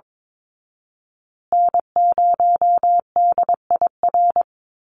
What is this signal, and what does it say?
Morse code, heard as a keyed tone, about 22 wpm, 700 Hz. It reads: N0DIR
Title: Call Signs Repeated 2x - 22wpm